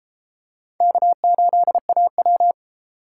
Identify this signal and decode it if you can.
Morse code, heard as a keyed tone, about 33 wpm, 700 Hz. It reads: K8AW